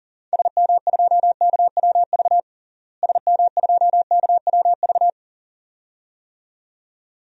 Morse code, heard as a keyed tone, about 40 wpm, 700 Hz. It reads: SM2KWV SM2KWV
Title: DXCC-100 Call Signs - 6 Characters - Repeated 2x - 40wpm